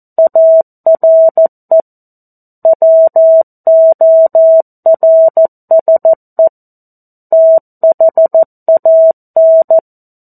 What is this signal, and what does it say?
Morse code, heard as a keyed tone, about 14 wpm, 650 Hz. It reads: ARE WORSE THAN
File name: 02-Chapter-02 14 WPM